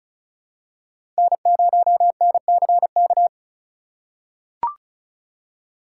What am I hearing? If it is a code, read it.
Morse code, heard as a keyed tone, about 36 wpm, 700 Hz. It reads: N0NCK
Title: USA Call Signs - 5 Characters - 2x Character Spacing - 35wpm